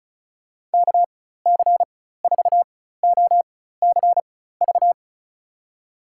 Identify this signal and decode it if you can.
Morse code, heard as a keyed tone, about 36 wpm, spaced as for 16 wpm, 700 Hz. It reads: KC4OCV
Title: USA Call Signs - 6 Characters - 4x Character Spacing - 35wpm